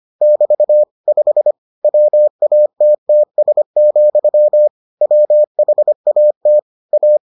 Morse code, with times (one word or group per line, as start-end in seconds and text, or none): 0.2-0.8 <BT>
1.1-1.5 5
1.8-4.7 WATTS,
5.0-6.6 WHAT
6.9-7.2 A